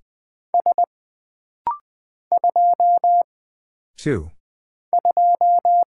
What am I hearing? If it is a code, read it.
Morse code, heard as a keyed tone, about 20 wpm, 700 Hz. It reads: S 2 2
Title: Single Character (Letter:Number:Punctuation:Prosign) - Rapid-Fire - 20wpm